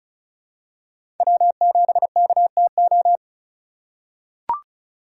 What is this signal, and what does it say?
Morse code, heard as a keyed tone, about 36 wpm, 700 Hz. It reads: W7KTO
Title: USA Call Signs - 5 Characters - 2x Character Spacing - 35wpm